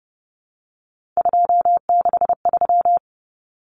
Morse code, heard as a keyed tone, about 30 wpm, 700 Hz. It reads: 263
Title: Numbers - 3-Digit 30wpm